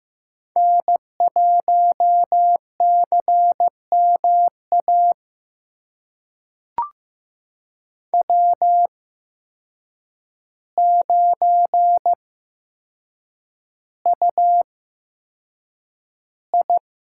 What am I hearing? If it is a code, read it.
Morse code, heard as a keyed tone, about 15 wpm, spaced as for 7 wpm, 700 Hz. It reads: N1CMA W9UI
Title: USA Call Signs - 8x Character Spacing - 15wpm